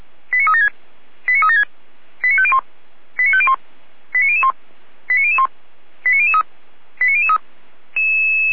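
Five-tone selective call (ZVEI), heard as quick runs of five tones, tones of about 70 ms. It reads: 89376 89376 89501 89501 89001 89001 89003 89003
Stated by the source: ZVEI five-tone call received on SDR.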